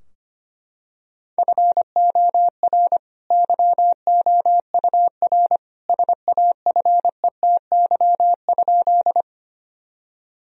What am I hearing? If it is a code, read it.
Morse code, heard as a keyed tone, about 25 wpm, 700 Hz. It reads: FOR YOUR SAFETY?
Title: Sentences from Top 1000 Words - Review Part 4 - 25wpm